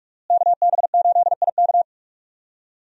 Morse code, heard as a keyed tone, about 41 wpm, 700 Hz. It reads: KB8IK
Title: USA Call Signs - 5 Characters - Repeated 3x - 45wpm